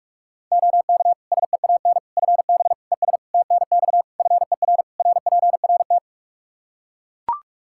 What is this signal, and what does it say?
Morse code, heard as a keyed tone, about 41 wpm, 700 Hz. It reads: OK SEAN FB ES TNX FER RPRT
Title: Practice QSO - Exchange 2 - Info - 45wpm